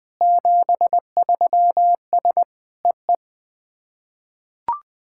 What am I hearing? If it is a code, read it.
Morse code, heard as a keyed tone, about 20 wpm, 690 Hz. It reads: 73S EE
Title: SOTA QSO Practice 20wpm